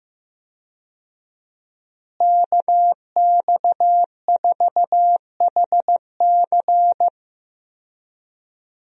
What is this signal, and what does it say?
Morse code, heard as a keyed tone, about 16 wpm, 700 Hz. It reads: KX4HC
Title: USA Call Signs - 5 Characters - Repeated 2x - 15wpm